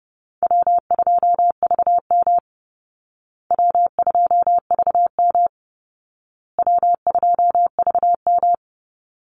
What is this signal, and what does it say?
Morse code, heard as a keyed tone, about 28 wpm, 700 Hz. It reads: W2VM W2VM W2VM
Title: Easy Call Signs Repeated 3x - 30wpm